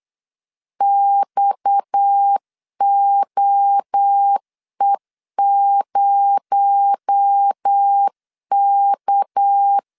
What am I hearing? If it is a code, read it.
Morse code, heard as a keyed tone, about 8 wpm, 800 Hz. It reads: XOE0K
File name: CW / DET / 1 6 1500C S B 8wpm 5142